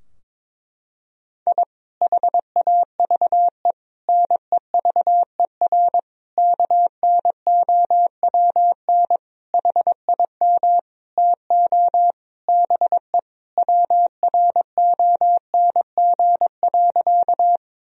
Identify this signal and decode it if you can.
Morse code, heard as a keyed tone, about 22 wpm, 700 Hz. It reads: I HAVE NEVER KNOWN HIM TO BE WRONG.
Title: Sentences from Top 1000 Words - Review Part 3 - 22wpm